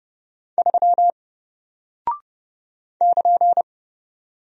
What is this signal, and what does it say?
Morse code, heard as a keyed tone, about 29 wpm, 700 Hz. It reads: 3 <KN>
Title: Single Character (Letter:Number:Punctuation:Prosign) Alt 30wpm